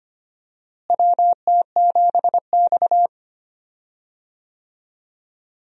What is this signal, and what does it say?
Morse code, heard as a keyed tone, about 25 wpm, 700 Hz. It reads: WT7X
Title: USA Call Signs - 4 Characters - Repeated 3x - 25wpm